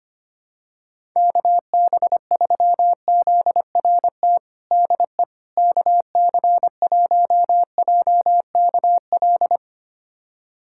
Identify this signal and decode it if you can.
Morse code, heard as a keyed tone, about 25 wpm, 700 Hz. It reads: KB3ZRT DE KC1JKL
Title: USA Callsign DE Callsign 25wpm